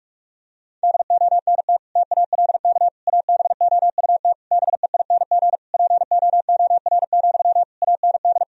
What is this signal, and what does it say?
Morse code, heard as a keyed tone, about 41 wpm, 700 Hz. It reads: DONT TALK ABOUT BEING POOR, AND
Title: Sentences from Top 1000 Words - Review Part 4 - 45wpm